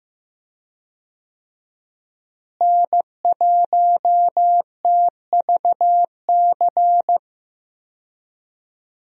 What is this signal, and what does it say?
Morse code, heard as a keyed tone, about 15 wpm, 700 Hz. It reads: N1TVC
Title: USA Call Signs - 5 Characters - Repeated 2x - 15wpm